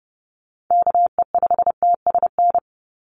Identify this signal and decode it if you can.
Morse code, heard as a keyed tone, about 30 wpm, 700 Hz. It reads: KE5TSN
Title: USA Call Signs 30wpm